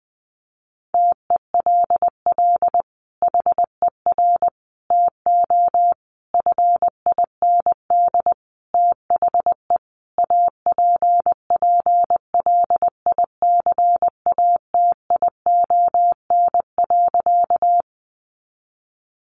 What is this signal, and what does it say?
Morse code, heard as a keyed tone, about 20 wpm, 700 Hz. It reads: TELL HER TO FIND THE APPLICATION.
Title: Sentences from Top 1200 Words 20wpm